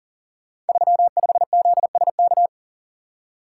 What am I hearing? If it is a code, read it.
Morse code, heard as a keyed tone, about 39 wpm, 700 Hz. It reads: 35ZSK
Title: Code Groups - 5 Letter:Numbers - 40wpm